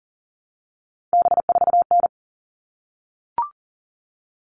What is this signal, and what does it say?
Morse code, heard as a keyed tone, about 40 wpm, 700 Hz. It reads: B4N
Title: All Cut Numbers - 3-Digit 40wpm